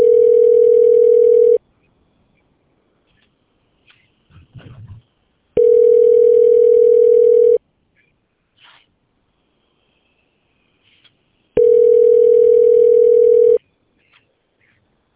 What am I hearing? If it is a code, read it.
Ringback tone (North America), 440 + 480 Hz.